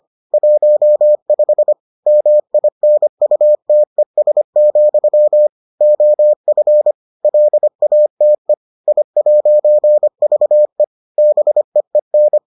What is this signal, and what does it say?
Morse code, heard as a keyed tone, about 25 wpm, 600 Hz. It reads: 15 MINUTES, OF LATE I'VE BEEN